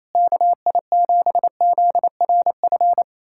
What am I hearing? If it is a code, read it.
Morse code, heard as a keyed tone, about 28 wpm, 700 Hz. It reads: KI7ZRF
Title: USA Callsign DE Callsign 28wpm